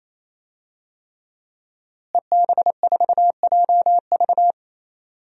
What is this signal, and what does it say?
Morse code, heard as a keyed tone, about 28 wpm, 700 Hz. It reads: EB4JV
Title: DXCC-100 Call Signs - 5 Characters - Repeated 3x - 28wpm